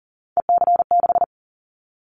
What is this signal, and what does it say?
Morse code, heard as a keyed tone, about 38 wpm, 700 Hz. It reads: EC6
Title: Callsign Prefixes for Top 50 DXCC - 40wpm